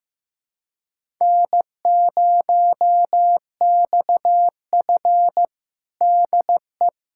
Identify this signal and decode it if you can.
Morse code, heard as a keyed tone, about 16 wpm, 700 Hz. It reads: N0XF DE
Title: USA Callsign DE Callsign - 4 Characters - 15wpm